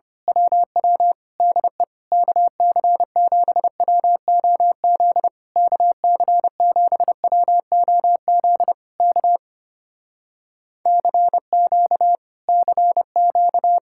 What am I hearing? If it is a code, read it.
Morse code, heard as a keyed tone, about 24 wpm, 700 Hz. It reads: WW DE KC7WOZ KC7WOZ K CQ CQ
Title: Speed-Racing Common QSO Phrases - 25wpm